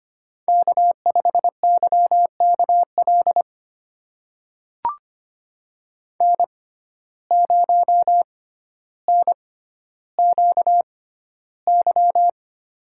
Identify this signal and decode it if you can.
Morse code, heard as a keyed tone, about 25 wpm, spaced as for 13 wpm, 700 Hz. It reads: K5YKL N0NQY
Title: USA Call Signs - 5 Characters - 6x Character Spacing - 25wpm